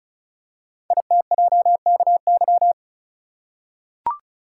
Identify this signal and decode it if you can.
Morse code, heard as a keyed tone, about 36 wpm, 700 Hz. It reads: ITJKY
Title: Code Groups - 5 Letters - 35wpm